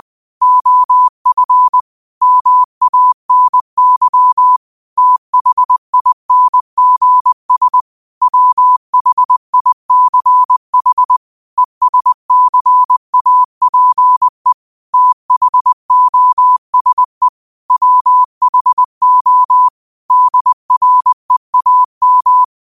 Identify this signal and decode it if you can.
Morse code, heard as a keyed tone, about 20 wpm, 1000 Hz. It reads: OF MANY THINGS WHICH ESCAPE THOSE WHO DREAM
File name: qotdfiles / 20wpm